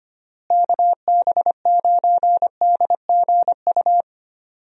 Morse code, heard as a keyed tone, about 25 wpm, 700 Hz. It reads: KB9DGU
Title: USA Call Signs - 6 Characters - Repeated 3x - 25wpm